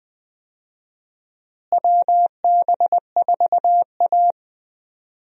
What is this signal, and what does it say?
Morse code, heard as a keyed tone, about 21 wpm, 700 Hz. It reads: WB4A
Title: USA Call Signs - 4 Characters - Repeated 3x - 20wpm